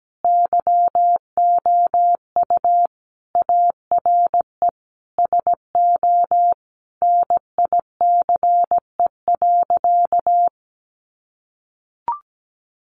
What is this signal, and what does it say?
Morse code, heard as a keyed tone, about 17 wpm, 700 Hz. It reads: YOU ARE SO NICE.
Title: Sentences from Top 1100 Words 17wpm